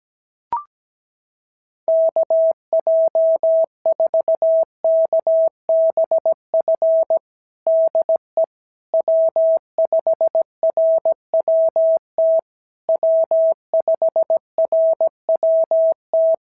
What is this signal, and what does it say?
Morse code, heard as a keyed tone, about 17 wpm, 650 Hz. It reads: KJ4KBF DE W5RWT W5RWT
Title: SOTA QSO Practice 17wpm